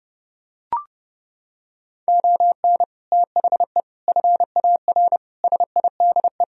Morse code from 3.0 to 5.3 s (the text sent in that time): THE FAR